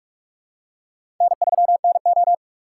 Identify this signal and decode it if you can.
Morse code, heard as a keyed tone, about 43 wpm, 700 Hz. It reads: N3NO
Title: USA Call Signs - 4 Characters - 6x Character Spacing - 45wpm